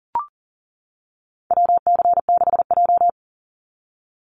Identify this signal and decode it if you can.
Morse code, heard as a keyed tone, about 38 wpm, 700 Hz. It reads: WC6J
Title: NA SOTA Activator Callsigns 40wpm